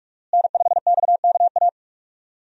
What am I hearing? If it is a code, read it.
Morse code, heard as a keyed tone, about 46 wpm, 700 Hz. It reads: N5XKA
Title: USA Call Signs - 5 Characters - Repeated 2x - 45wpm